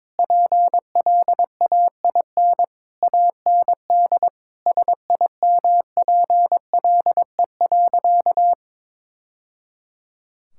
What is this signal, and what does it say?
Morse code, heard as a keyed tone, about 22 wpm, 700 Hz. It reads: PLAIN AND SIMPLE.
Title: Sentences from Top 1000 Words - Review Part 5 - 22wpm